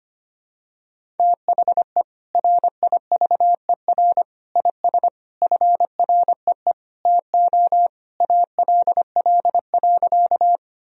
Morse code, heard as a keyed tone, about 25 wpm, 700 Hz. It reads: THE RIVER IS FREE TO ALL.
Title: Sentences from Top 1000 Words - Review Part 1 - 25wpm